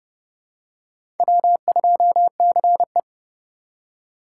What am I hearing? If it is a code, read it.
Morse code, heard as a keyed tone, about 30 wpm, 700 Hz. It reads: W2CE
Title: USA Call Signs - 4 Characters - 4x Character Spacing - 30wpm